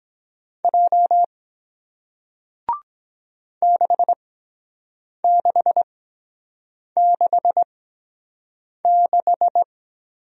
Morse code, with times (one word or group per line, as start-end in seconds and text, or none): 0.6-1.2 J
3.6-9.6 6666